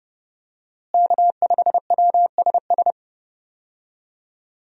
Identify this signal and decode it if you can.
Morse code, heard as a keyed tone, about 31 wpm, 700 Hz. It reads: K5WSS
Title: USA Call Signs - 5 Characters - Repeated 2x - 30wpm